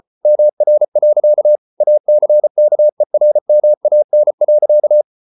Morse code, heard as a keyed tone, about 34 wpm, 600 Hz. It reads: MR. ACKERMAN.